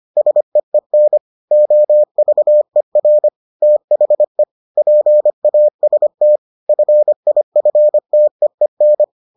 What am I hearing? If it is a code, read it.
Morse code, heard as a keyed tone, about 25 wpm, 600 Hz. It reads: SEEN OVER THE PAST FIFTEEN